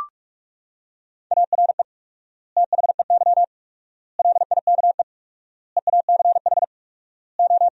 Morse code, heard as a keyed tone, about 43 wpm, 700 Hz. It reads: ARE THEY LIKE EACH O
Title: 4x Word-Spacing - Sentences from Top 200 Words - 45wpm